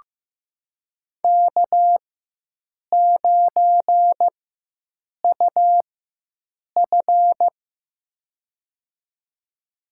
Morse code, heard as a keyed tone, about 15 wpm, spaced as for 7 wpm, 700 Hz. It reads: K9UF
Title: USA Call Signs - 4 Characters - 4x Character Spacing - 15wpm